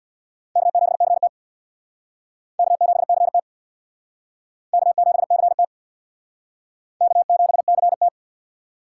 Morse code, heard as a keyed tone, about 65 wpm, 700 Hz. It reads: K7CT K7CT K7CT K7CT